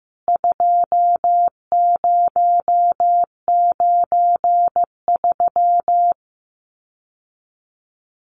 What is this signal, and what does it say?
Morse code, heard as a keyed tone, about 15 wpm, 700 Hz. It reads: 2093
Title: CWT Exchange 15wpm